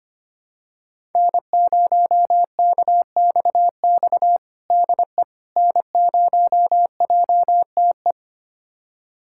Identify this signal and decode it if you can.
Morse code, heard as a keyed tone, about 25 wpm, 700 Hz. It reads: N0KXX DE N0JTE
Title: USA Callsign DE Callsign - 5 Characters - 25wpm